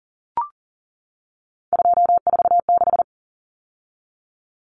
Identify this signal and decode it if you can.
Morse code, heard as a keyed tone, about 40 wpm, 700 Hz. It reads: 246
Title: Numbers - 3-Digit 40wpm